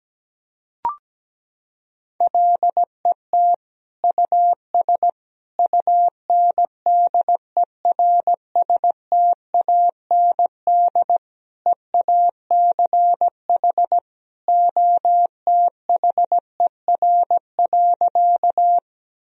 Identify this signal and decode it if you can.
Morse code, heard as a keyed tone, about 17 wpm, 700 Hz. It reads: LET US UNDERSTAND EACH OTHER.
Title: Sentences from Top 1000 Words - Review Part 1 - 17wpm